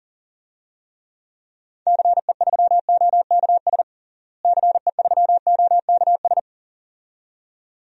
Morse code, heard as a keyed tone, about 37 wpm, 700 Hz. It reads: CE3OKS CE3OKS
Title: DXCC-100 Call Signs - 6 Characters - Repeated 2x - 40wpm